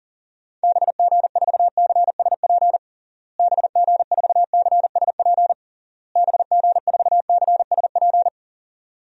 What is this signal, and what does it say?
Morse code, heard as a keyed tone, about 37 wpm, 700 Hz. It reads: BG4CSP BG4CSP BG4CSP
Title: DXCC-100 Call Signs - 6 Characters - Repeated 3x - 40wpm